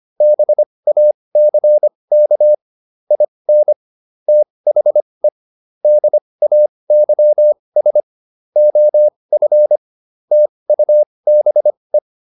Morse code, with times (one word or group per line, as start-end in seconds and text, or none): 0.2-2.5 BACK
3.1-3.7 IN
4.3-5.3 THE
5.8-8.0 DAYS
8.6-9.8 OF
10.3-12.0 TUBE